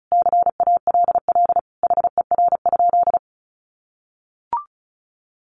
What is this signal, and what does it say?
Morse code, heard as a keyed tone, about 34 wpm, 700 Hz. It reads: CALL HER?
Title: Sentences from Top 100 Words 35wpm